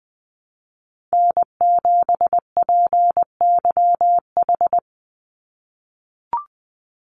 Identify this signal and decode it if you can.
Morse code, heard as a keyed tone, about 19 wpm, 700 Hz. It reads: N7PYH
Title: USA Call Signs 20wpm